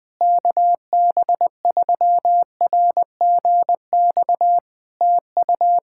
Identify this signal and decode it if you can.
Morse code, heard as a keyed tone, about 20 wpm, 700 Hz. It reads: KB3RGX TU
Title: Practice QSO - Exchange 4 - Ending - 20wpm